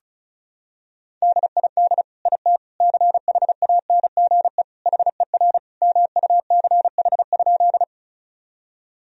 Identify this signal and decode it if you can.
Morse code, heard as a keyed tone, about 36 wpm, 700 Hz. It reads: DID IT CHANGE HER MUCH?
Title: Sentences from Top 1000 Words - Review Part 1 - 35wpm